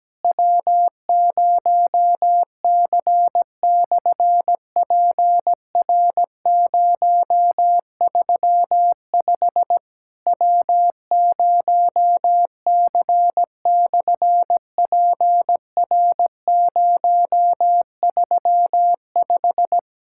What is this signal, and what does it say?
Morse code, heard as a keyed tone, about 17 wpm, 700 Hz. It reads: W0C/PR035 W0C/PR035
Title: SOTA QSO Practice 17wpm